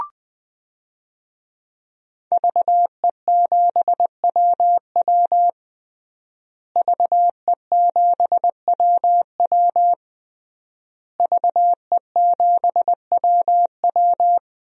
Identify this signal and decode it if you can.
Morse code, heard as a keyed tone, about 20 wpm, 700 Hz. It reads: VE7WW VE7WW VE7WW
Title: DXCC-100 Call Signs - 5 Characters - Repeated 3x - 20wpm